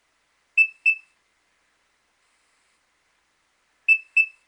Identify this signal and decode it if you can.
There is no code.